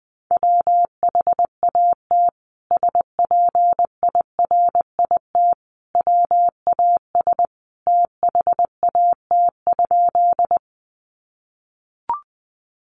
Morse code, heard as a keyed tone, about 20 wpm, 700 Hz. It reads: WHAT SPIRIT WAS THAT?
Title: Sentences from Top 1400 Words 20wpm